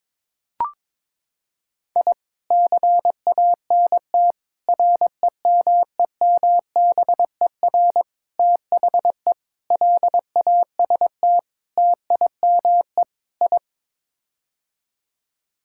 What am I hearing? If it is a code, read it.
Morse code, heard as a keyed tone, about 22 wpm, 700 Hz. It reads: I CANT REMEMBER THE LAST TIME I
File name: practice / 22wpm / Sets of 7 Words - 22wpm